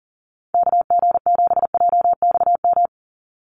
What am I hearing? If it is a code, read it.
Morse code, heard as a keyed tone, about 40 wpm, 700 Hz. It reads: KG7JXM
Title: USA Call Signs 40wpm